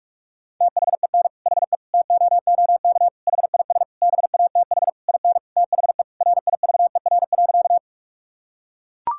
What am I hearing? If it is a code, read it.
Morse code, heard as a keyed tone, about 41 wpm, 700 Hz. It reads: THEN HE TOOK HIS BATH IN THE RIVER.
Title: Sentences from Top 1600 Words 45wpm